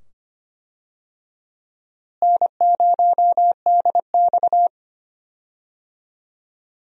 Morse code, heard as a keyed tone, about 25 wpm, 700 Hz. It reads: N0DX
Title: USA Call Signs - 4 Characters - Repeated 2x - 25wpm